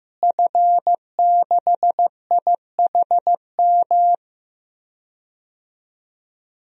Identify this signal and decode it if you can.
Morse code, heard as a keyed tone, about 15 wpm, 700 Hz. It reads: F6IHM
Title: DXCC-100 Call Signs - 5 Characters - Repeated 3x - 15wpm